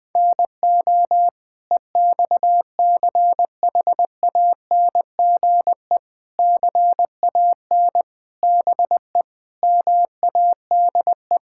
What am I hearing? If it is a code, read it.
Morse code, heard as a keyed tone, about 20 wpm, 700 Hz. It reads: NO EXCHANGE CAN BE MADE
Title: Sentences from Top 1000 Words - Review Part 6 - 20wpm